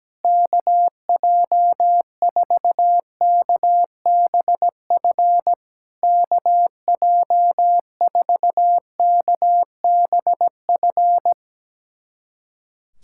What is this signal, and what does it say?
Morse code, heard as a keyed tone, about 17 wpm, 700 Hz. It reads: KJ4KBF KJ4KBF
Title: SOTA QSO Practice 17wpm